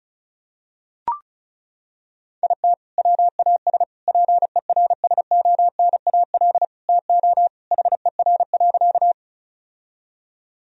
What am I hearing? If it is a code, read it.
Morse code, heard as a keyed tone, about 36 wpm, 700 Hz. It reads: IT WAS PERSONAL TO HER.
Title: Sentences from Top 1000 Words - Review Part 4 - 35wpm